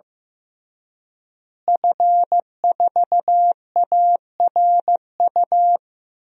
Morse code, heard as a keyed tone, about 15 wpm, 700 Hz. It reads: F4ARU
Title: DXCC-100 Call Signs - 5 Characters - Repeated 3x - 15wpm